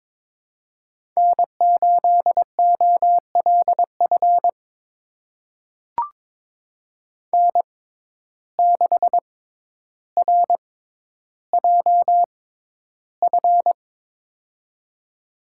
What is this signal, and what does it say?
Morse code, heard as a keyed tone, about 22 wpm, spaced as for 11 wpm, 700 Hz. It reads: N8OLF N6RJF
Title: USA Call Signs - 5 Characters - 6x Character Spacing - 22wpm